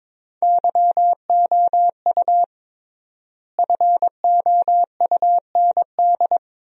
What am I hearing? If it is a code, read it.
Morse code, heard as a keyed tone, about 22 wpm, 700 Hz. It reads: YOU FOUND
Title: 3x Word-Spacing - Sentences from Top 300 Words - 22wpm